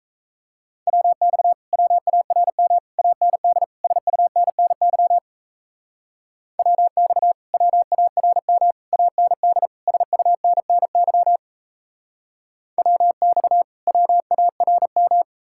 Morse code, heard as a keyed tone, about 38 wpm, 700 Hz. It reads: WX WARM AND SUNNY WX WARM AND SUNNY WX WARM